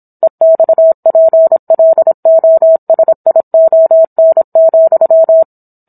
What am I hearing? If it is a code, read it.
Morse code, heard as a keyed tone, about 26 wpm, 650 Hz. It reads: EXPLOSION,